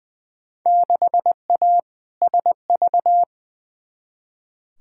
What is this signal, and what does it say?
Morse code, heard as a keyed tone, about 20 wpm, 700 Hz. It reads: 6A SV
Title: Field Day Exchanges Encore - 20wpm